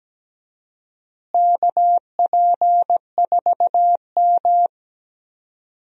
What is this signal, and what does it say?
Morse code, heard as a keyed tone, about 17 wpm, 700 Hz. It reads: KP4M
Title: DXCC-100 Call Signs - 4 Characters - Repeated 2x - 17wpm